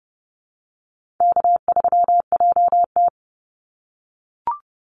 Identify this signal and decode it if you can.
Morse code, heard as a keyed tone, about 30 wpm, 700 Hz. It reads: K3JT
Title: Easy Call Signs - 30wpm